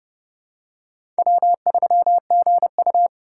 Morse code, heard as a keyed tone, about 30 wpm, 700 Hz. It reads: W3GU
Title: USA Call Signs - 4 Characters - 2x Character Spacing - 30wpm